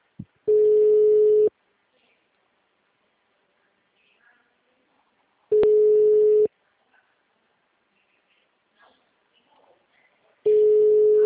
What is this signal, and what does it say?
Ringback tone (Europe, 425 Hz).